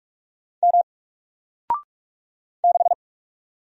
Morse code, heard as a keyed tone, about 45 wpm, 700 Hz. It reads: M 6
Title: Single Letter-Number Alt 45wpm